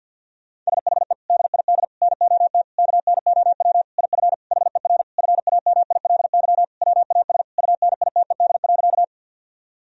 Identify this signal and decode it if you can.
Morse code, heard as a keyed tone, about 50 wpm, 700 Hz. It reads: SHE DID NOT KNOW IF HER FAMILY WAS UNITED.
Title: Sentences from Top 1000 Words - Review Part 4 - 50wpm